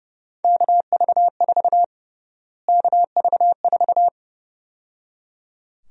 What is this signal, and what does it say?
Morse code, heard as a keyed tone, about 30 wpm, 700 Hz. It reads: KV4 KV4
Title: USA Call Sign Prefixes - Repeated 2x - 30wpm